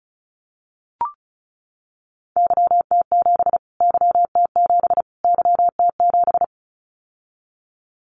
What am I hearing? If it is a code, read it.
Morse code, heard as a keyed tone, about 34 wpm, 700 Hz. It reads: YT7 YT7 YT7
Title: Callsign Prefixes for Top 25 DXCC - Repeated 3x - 35wpm